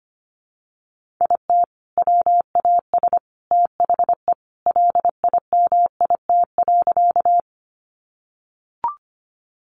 Morse code, heard as a keyed tone, about 25 wpm, 700 Hz. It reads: IT WAS THE LIMIT.
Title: Sentences from Top 1500 Words 25wpm